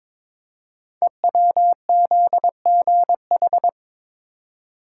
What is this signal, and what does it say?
Morse code, heard as a keyed tone, about 22 wpm, 700 Hz. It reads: EWZGH
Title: Code Groups - 5 Letters - 22wpm